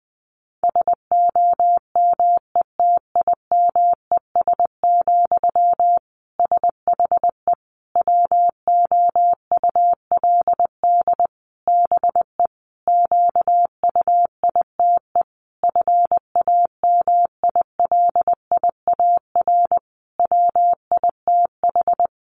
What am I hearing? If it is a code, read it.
Morse code, heard as a keyed tone, about 20 wpm, 700 Hz. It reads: SOMETIMES, SHE WOULD BE QUITE FAMILIAR WITH
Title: Sentences from Top 1000 Words 20wpm